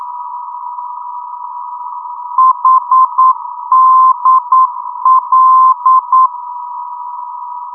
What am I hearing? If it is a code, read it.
Morse code, heard as a keyed tone, about 9 wpm, 1040 Hz. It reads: HDL
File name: europe / d-8060-cw-id-2004-rb